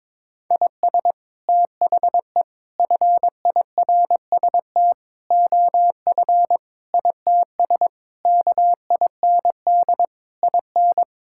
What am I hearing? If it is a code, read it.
Morse code, heard as a keyed tone, about 22 wpm, 700 Hz. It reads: IS THE FIRST OF ITS KIND IN
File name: practice / 22wpm / Sets of 8 Words - 22wpm